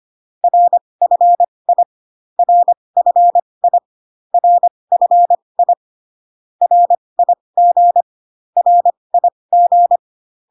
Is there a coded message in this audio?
Yes: Morse code from 0.4 to 10.0 s, heard as a keyed tone, about 25 wpm, spaced as for 20 wpm, 700 Hz. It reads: RFI RFI RFI RIG RIG